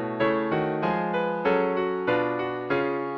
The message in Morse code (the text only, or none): none